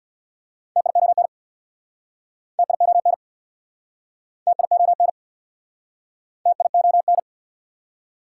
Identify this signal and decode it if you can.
Morse code, heard as a keyed tone, about 69 wpm, 700 Hz. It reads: TION TION TION TION